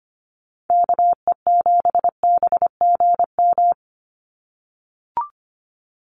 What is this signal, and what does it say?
Morse code, heard as a keyed tone, about 25 wpm, 690 Hz. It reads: KE7BGM
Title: NA SOTA Activator Callsigns 25wpm